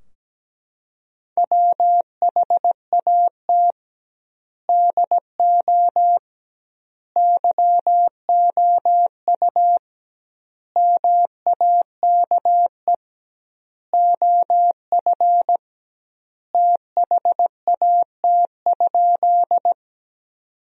Morse code, heard as a keyed tone, about 17 wpm, 700 Hz. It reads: WHAT DO YOU MAKE OF THAT?
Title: 2x Word-Spacing - Sentences from Top 100 Words - 17wpm